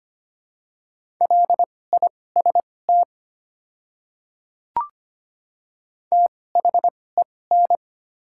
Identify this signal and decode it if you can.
Morse code, heard as a keyed tone, about 25 wpm, spaced as for 18 wpm, 700 Hz. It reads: LIST THEN
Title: 2x Character Spacing - Top 500 Words - 25wpm